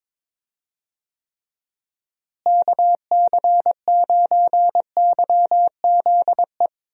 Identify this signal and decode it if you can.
Morse code, heard as a keyed tone, about 22 wpm, 700 Hz. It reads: KC9YZE
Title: USA Call Signs - 6 Characters - Repeated 3x - 22wpm